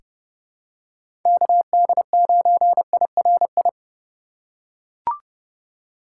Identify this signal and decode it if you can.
Morse code, heard as a keyed tone, about 30 wpm, 700 Hz. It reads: KD9IRI
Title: USA Call Signs - 6 Characters - 2x Character Spacing - 30wpm